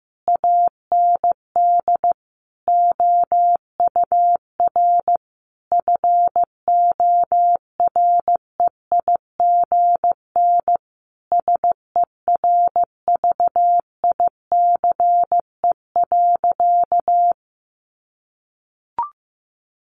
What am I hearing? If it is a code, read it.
Morse code, heard as a keyed tone, about 15 wpm, 700 Hz. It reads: AND OUR FOREIGN SERVICE.
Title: Sentences from Top 1400 Words 15wpm